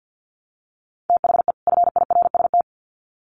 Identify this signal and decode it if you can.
Morse code, heard as a keyed tone, about 50 wpm, 700 Hz. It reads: THE FIRST
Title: Sets of 2 Words 50wpm